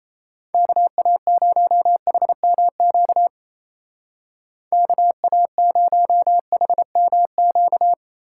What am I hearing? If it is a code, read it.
Morse code, heard as a keyed tone, about 33 wpm, 700 Hz. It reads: KA0HMQ KA0HMQ
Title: Speed Racing - USA 2x3 Call Signs 28wpm